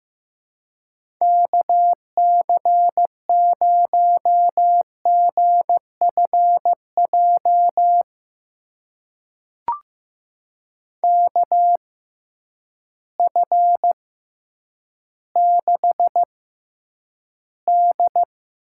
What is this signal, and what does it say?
Morse code, heard as a keyed tone, about 15 wpm, spaced as for 9 wpm, 700 Hz. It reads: KC0GFJ KF6D
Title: USA Call Signs - 6 Characters - 6x Character Spacing - 15wpm